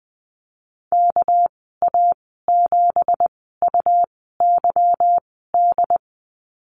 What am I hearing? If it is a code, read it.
Morse code, heard as a keyed tone, about 20 wpm, spaced as for 14 wpm, 700 Hz. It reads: KA7UYD
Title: 207 Lesson 33g - Letter j - Call Signs 20wpm